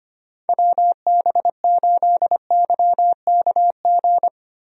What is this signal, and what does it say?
Morse code, heard as a keyed tone, about 25 wpm, 700 Hz. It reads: WB8YKG
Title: USA Call Signs - 6 Characters - Repeated 2x - 25wpm